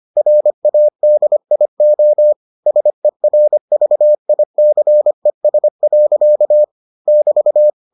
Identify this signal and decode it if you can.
Morse code, heard as a keyed tone, about 25 wpm, 600 Hz. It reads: RADIO SERVICES. <BT>